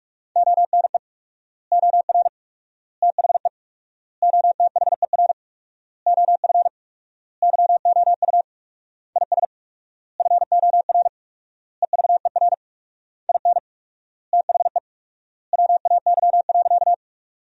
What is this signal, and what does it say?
Morse code, heard as a keyed tone, about 46 wpm, 700 Hz. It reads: ONE OR THE OTHER OF YOU IS FOR EVER IN THE WAY.
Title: 4x Word-Spacing - Sentences from Top 400 Words - 45wpm